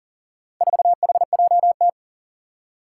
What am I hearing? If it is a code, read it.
Morse code, heard as a keyed tone, about 38 wpm, 700 Hz. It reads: 4HJT
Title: Four Challenging Letter-Number Combinations 40wpm